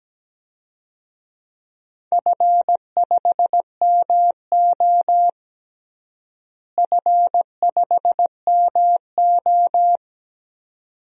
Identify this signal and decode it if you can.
Morse code, heard as a keyed tone, about 17 wpm, 700 Hz. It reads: F5MO F5MO
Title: DXCC-100 Call Signs - 4 Characters - Repeated 2x - 17wpm